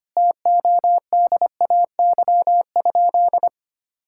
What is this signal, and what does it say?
Morse code, heard as a keyed tone, about 25 wpm, 700 Hz. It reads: TODAY?
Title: Sentences from Top 1600 Words 25wpm